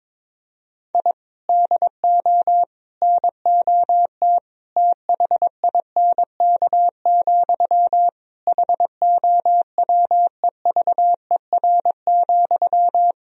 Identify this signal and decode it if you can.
Morse code, heard as a keyed tone, about 22 wpm, 700 Hz. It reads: I DO NOT THINK, HOWEVER,
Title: Sentences from Top 1000 Words - Review Part 3 - 22wpm